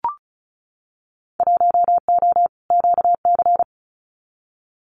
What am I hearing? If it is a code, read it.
Morse code, heard as a keyed tone, about 34 wpm, 700 Hz. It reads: 1O QC
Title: Winter Field Day Exchanges 35wpm